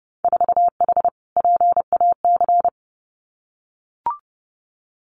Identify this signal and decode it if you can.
Morse code, heard as a keyed tone, about 30 wpm, 700 Hz. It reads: 4H PAC
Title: Winter Field Day Exchanges Encore 30wpm